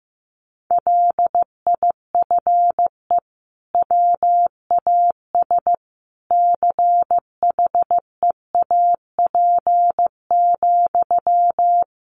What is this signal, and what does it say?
Morse code, heard as a keyed tone, about 15 wpm, 700 Hz. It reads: LIFE WAS CHEAP,
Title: Sentences from Top 1200 Words 15wpm